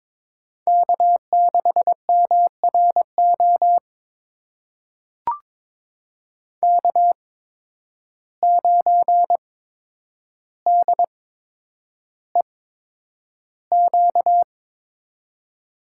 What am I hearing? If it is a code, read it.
Morse code, heard as a keyed tone, about 22 wpm, spaced as for 9 wpm, 700 Hz. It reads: K6MRO K9DEQ
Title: USA Call Signs - 8x Character Spacing - 22wpm